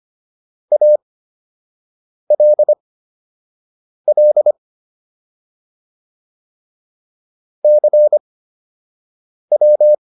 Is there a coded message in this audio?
Yes: Morse code from 0.7 to 9.9 s, heard as a keyed tone, about 25 wpm, spaced as for 6 wpm, 600 Hz. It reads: ALL CW